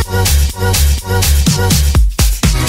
There is no code.